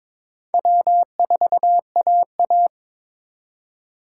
Morse code, heard as a keyed tone, about 26 wpm, 700 Hz. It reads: W4AA
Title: Speed Racing - USA 1x2 Call Signs 22wpm